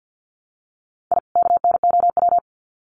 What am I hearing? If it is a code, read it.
Morse code, heard as a keyed tone, about 50 wpm, 700 Hz. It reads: I KNOW
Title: Sets of 2 Words 50wpm